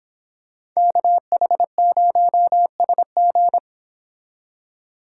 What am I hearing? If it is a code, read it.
Morse code, heard as a keyed tone, about 26 wpm, 700 Hz. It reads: KH0SG